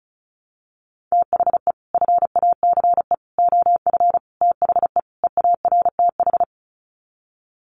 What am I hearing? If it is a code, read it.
Morse code, heard as a keyed tone, about 34 wpm, 700 Hz. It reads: THE FACE OF THE EARTH
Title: Sets of 5 Words Encore 35wpm